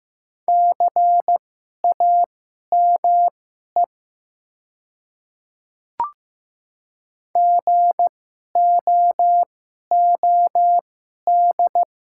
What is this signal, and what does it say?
Morse code, heard as a keyed tone, about 15 wpm, spaced as for 11 wpm, 700 Hz. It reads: CAME GOOD
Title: 2x Character Spacing - Top 200 Words - 15wpm